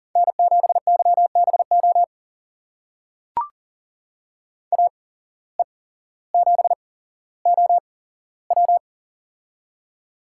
Morse code, heard as a keyed tone, about 39 wpm, spaced as for 17 wpm, 700 Hz. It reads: N7YBO AE7OW
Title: USA Call Signs - 5 Characters - 8x Character Spacing - 40wpm